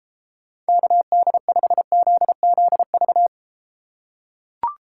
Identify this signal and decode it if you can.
Morse code, heard as a keyed tone, about 32 wpm, 700 Hz. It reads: KD5ZZV